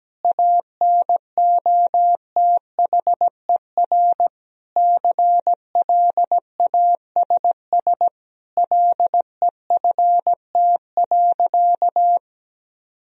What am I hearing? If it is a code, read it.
Morse code, heard as a keyed tone, about 17 wpm, 700 Hz. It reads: ANOTHER CLASS LEFT.
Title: Sentences from Top 1000 Words - Review Part 2 - 17wpm